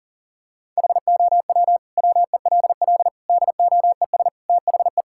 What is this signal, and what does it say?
Morse code, heard as a keyed tone, about 40 wpm, 700 Hz. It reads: HOW WELL DOES THE
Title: Sentences from Top 1000 Words - Review Part 6 - 40wpm